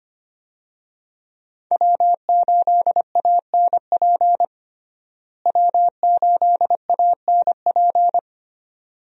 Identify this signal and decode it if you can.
Morse code, heard as a keyed tone, about 25 wpm, 700 Hz. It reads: W8ANP W8ANP
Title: USA Call Signs - 5 Characters - Repeated 2x - 25wpm